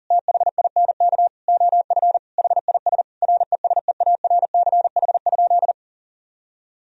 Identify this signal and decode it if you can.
Morse code, heard as a keyed tone, about 40 wpm, 700 Hz. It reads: THINK OF HIS RESEARCH?
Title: Sentences from Top 1000 Words - Review Part 4 - 40wpm